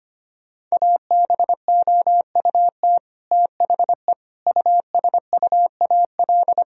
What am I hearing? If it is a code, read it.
Morse code, heard as a keyed tone, about 25 wpm, 700 Hz. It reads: ABOUT THE USUAL
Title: Sentences from Top 1000 Words - Review Part 2 - 25wpm